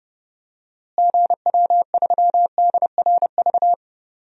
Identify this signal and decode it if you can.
Morse code, heard as a keyed tone, about 31 wpm, 700 Hz. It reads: GW3DRV
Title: DXCC-100 Call Signs - 6 Characters - Repeated 3x - 30wpm